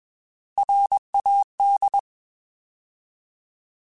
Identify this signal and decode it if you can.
Morse code, heard as a keyed tone, about 21 wpm, 800 Hz. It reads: RAD
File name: Alphabetical / Alphabetical mp3s / Rs @20